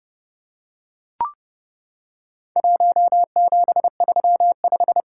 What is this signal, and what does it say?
Morse code, heard as a keyed tone, about 32 wpm, 700 Hz. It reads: 1735
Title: Worked All Europe DX Contest - QTC Exchange - 30wpm